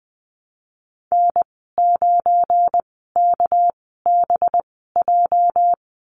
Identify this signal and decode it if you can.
Morse code, heard as a keyed tone, about 20 wpm, spaced as for 14 wpm, 700 Hz. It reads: N9KBJ
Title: 231 Lesson 37g - Letter x - Call Signs 20wpm